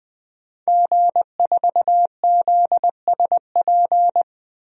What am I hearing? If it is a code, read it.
Morse code, heard as a keyed tone, about 20 wpm, 700 Hz. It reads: G4ZSP
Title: DXCC-100 Call Signs - 5 Characters - 20wpm